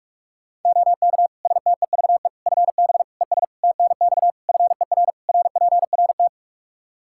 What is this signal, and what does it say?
Morse code, heard as a keyed tone, about 46 wpm, 700 Hz. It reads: OK STEVE FB ES TNX FER RPRT
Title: Practice QSO - Exchange 2 - Info - 45wpm